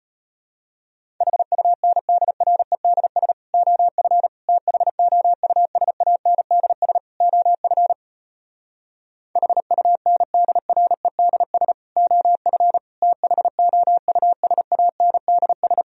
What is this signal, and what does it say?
Morse code, heard as a keyed tone, about 33 wpm, 700 Hz. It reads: HUNDREDS OF THOUSANDS OF HUNDREDS OF THOUSANDS